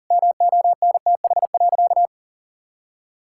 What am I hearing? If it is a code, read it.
Morse code, heard as a keyed tone, about 40 wpm, 700 Hz. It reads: MONTH.
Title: Sentences from Top 1000 Words - Review Part 6 - 40wpm